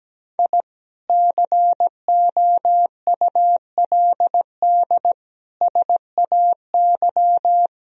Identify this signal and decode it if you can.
Morse code, heard as a keyed tone, about 17 wpm, 700 Hz. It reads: I COULD SAY
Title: Sentences from Top 1000 Words - Review Part 2 - 17wpm